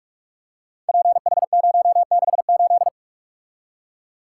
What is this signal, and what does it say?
Morse code, heard as a keyed tone, about 44 wpm, 700 Hz. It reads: PH068
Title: Code Groups - 5 Letter:Numbers - 45wpm